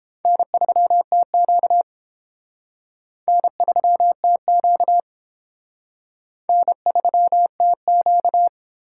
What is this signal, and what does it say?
Morse code, heard as a keyed tone, about 29 wpm, 700 Hz. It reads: N3TQ N3TQ N3TQ